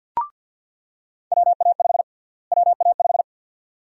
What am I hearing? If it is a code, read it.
Morse code, heard as a keyed tone, about 46 wpm, 700 Hz. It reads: WA5 WA5
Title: USA Call Sign Prefixes - Repeated 2x - 50wpm